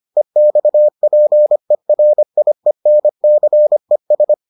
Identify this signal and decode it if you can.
Morse code, heard as a keyed tone, about 25 wpm, 600 Hz. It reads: EXPERIENCES